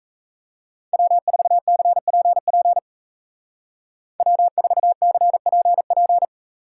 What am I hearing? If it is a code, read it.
Morse code, heard as a keyed tone, about 34 wpm, 700 Hz. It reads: W4CPP W4CPP
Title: Speed Racing - USA 1x3 Call Signs 28wpm